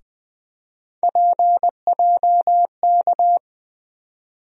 Morse code, heard as a keyed tone, about 20 wpm, 700 Hz. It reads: PJK
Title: USA Call Sign Postfixes - 3 Letters 20wpm